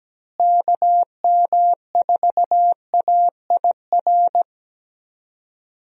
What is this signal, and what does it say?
Morse code, heard as a keyed tone, about 17 wpm, 700 Hz. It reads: KM4AIR
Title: USA Call Signs - 6 Characters - Repeated 2x - 17wpm